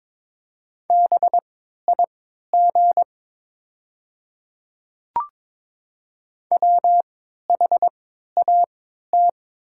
Morse code, heard as a keyed tone, about 22 wpm, spaced as for 12 wpm, 690 Hz. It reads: BIG WHAT
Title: 3x Character Spacing - Top 300 Words - 22wpm